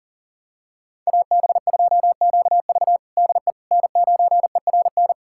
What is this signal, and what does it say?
Morse code, heard as a keyed tone, about 37 wpm, 690 Hz. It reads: AB2QV DE N9ERN
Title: USA Callsign DE Callsign - 5 Characters - 40wpm